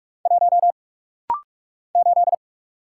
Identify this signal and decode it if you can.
Morse code, heard as a keyed tone, about 44 wpm, 700 Hz. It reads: 1 8
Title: Single Letter-Number - Rapid-Fire 45wpm